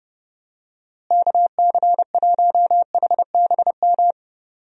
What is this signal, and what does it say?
Morse code, heard as a keyed tone, about 31 wpm, 700 Hz. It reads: KC1HBM
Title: USA Call Signs - 6 Characters - Repeated 3x - 30wpm